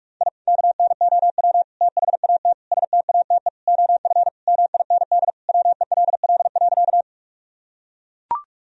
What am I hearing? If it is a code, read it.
Morse code, heard as a keyed tone, about 42 wpm, 700 Hz. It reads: I KNOW THAT STATE OF MIND WELL.
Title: Sentences from Top 1000 Words - Review Part 2 - 45wpm